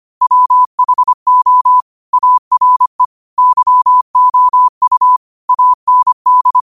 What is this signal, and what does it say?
Morse code, heard as a keyed tone, about 25 wpm, 1000 Hz. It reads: WHO ARE YOU AND
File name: qotdfiles / 25wpm